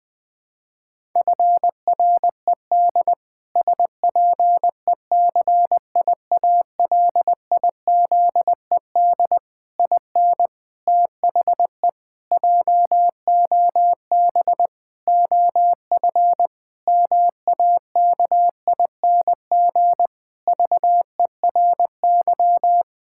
Morse code, heard as a keyed tone, about 20 wpm, 700 Hz. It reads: FRED SPECIALIZED IN THE JOB OF MAKING VERY
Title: Pangram Challenge v2 - 20wpm